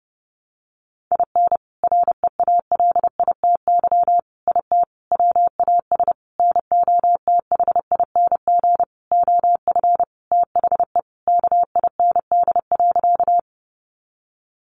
Morse code, heard as a keyed tone, about 30 wpm, 700 Hz. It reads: IN REALITY IT WAS NOTHING OF THE KIND.
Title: Sentences from Top 1200 Words 30wpm